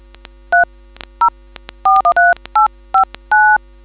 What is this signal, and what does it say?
DTMF, heard as telephone touch tones, tones of about 140 ms. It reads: 3 0413859